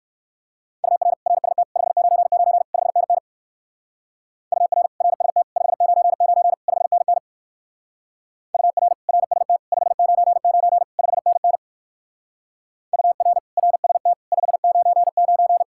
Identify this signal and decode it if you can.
Morse code, heard as a keyed tone, about 55 wpm, 700 Hz. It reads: UR RST 599 5NN UR RST 599 5NN UR RST 599 5NN UR RST 599
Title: Speed-Racing Common QSO Phrases - 45wpm